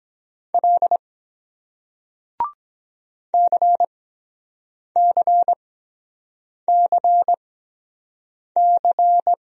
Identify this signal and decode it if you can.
Morse code, heard as a keyed tone, about 22 wpm, spaced as for 7 wpm, 700 Hz. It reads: L CCCC